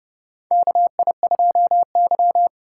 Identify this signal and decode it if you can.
Morse code, heard as a keyed tone, about 30 wpm, 700 Hz. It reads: KI2Y
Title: USA Call Signs - 4 Characters - 3x Character Spacing - 30wpm